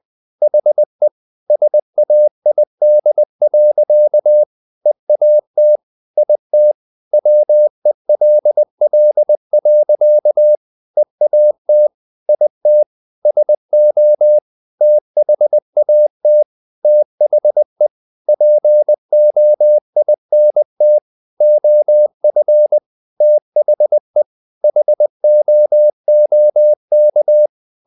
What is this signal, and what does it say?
Morse code, heard as a keyed tone, about 20 wpm, 600 Hz. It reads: HE SAID. EAT IT WELL. EAT IT SO THAT THE POINT OF THE HOOK
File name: the old man and the sea / the old man and the sea part 1 20wpm